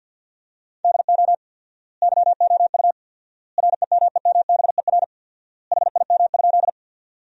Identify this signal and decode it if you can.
Morse code, heard as a keyed tone, about 51 wpm, 700 Hz. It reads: DO YOU REMEMBER HIM?
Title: 4x Word-Spacing - Sentences from Top 400 Words - 50wpm